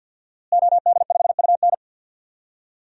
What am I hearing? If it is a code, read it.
Morse code, heard as a keyed tone, about 52 wpm, 700 Hz. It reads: OD5UN
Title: DXCC-100 Call Signs - 5 Characters - Repeated 3x - 50wpm